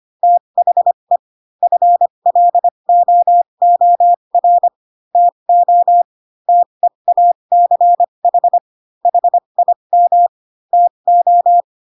Morse code, heard as a keyed tone, about 25 wpm, spaced as for 22 wpm, 700 Hz. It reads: THE FLOOR TO TEACH HIM TO
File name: Pinocchio / Pinocchio ch3 18 25 ik1hge